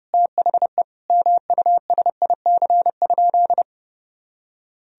Morse code, heard as a keyed tone, about 32 wpm, 700 Hz. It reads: THE MUSIC?
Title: Sentences from Top 1000 Words - Review Part 1 - 30wpm